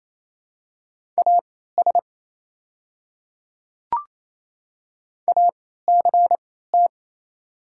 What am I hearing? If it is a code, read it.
Morse code, heard as a keyed tone, about 28 wpm, spaced as for 16 wpm, 700 Hz. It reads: AS ACT
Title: 3x Character Spacing - Top 500 Words - 28wpm